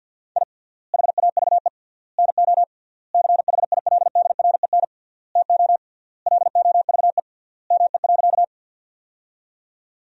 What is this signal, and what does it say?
Morse code, heard as a keyed tone, about 50 wpm, 700 Hz. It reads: I HAVE NO CHILDREN TO LOVE ME.
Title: 3x Word-Spacing - Sentences from Top 400 Words - 50wpm